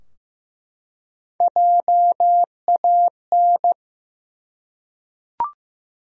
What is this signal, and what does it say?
Morse code, heard as a keyed tone, about 15 wpm, 700 Hz. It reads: JAN
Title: USA Common Names - Max 3 Letters - 15wpm